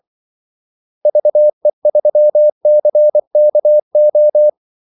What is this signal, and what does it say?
Morse code, heard as a keyed tone, about 24 wpm, 600 Hz. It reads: VE3CKO
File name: Callsigns-24wpm